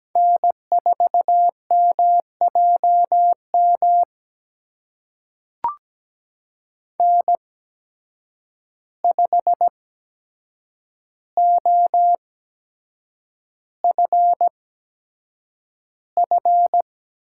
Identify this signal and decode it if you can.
Morse code, heard as a keyed tone, about 17 wpm, spaced as for 7 wpm, 700 Hz. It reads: N4MJM N5OFF
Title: USA Call Signs - 8x Character Spacing - 17wpm